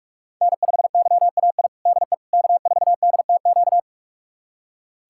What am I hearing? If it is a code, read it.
Morse code, heard as a keyed tone, about 46 wpm, 700 Hz. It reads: N5YAI DE K4DTQ
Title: USA Callsign DE Callsign 45wpm